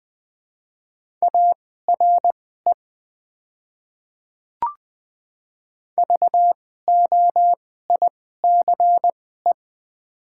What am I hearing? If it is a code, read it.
Morse code, heard as a keyed tone, about 20 wpm, spaced as for 15 wpm, 690 Hz. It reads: ARE VOICE
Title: 2x Character Spacing - Top 500 Words - 20wpm